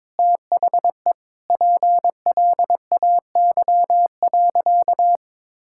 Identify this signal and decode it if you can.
Morse code, heard as a keyed tone, about 22 wpm, 700 Hz. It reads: THE PLAY.
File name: practice / 22wpm / Sentences with 4-Word Phrases - 22wpm